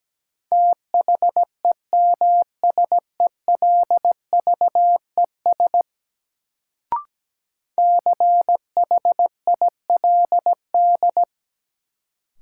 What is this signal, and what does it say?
Morse code, heard as a keyed tone, about 17 wpm, 700 Hz. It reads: THEMSELVES CHILD
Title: Sentences from Top 800 Words - New Word Drill - 17wpm